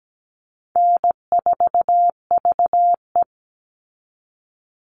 Morse code, heard as a keyed tone, about 17 wpm, 700 Hz. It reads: N4VE
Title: All Cut Numbers - 4-Digit 17wpm